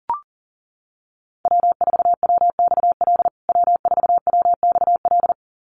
Morse code, heard as a keyed tone, about 40 wpm, 700 Hz. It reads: W4WXL W4WXL
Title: USA Call Signs Repeated 2x 40wpm